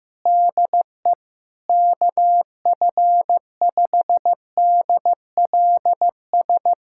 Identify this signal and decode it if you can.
Morse code, heard as a keyed tone, about 15 wpm, 700 Hz. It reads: DE KF5DLS
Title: USA Callsign DE Callsign - 6 Characters - 15wpm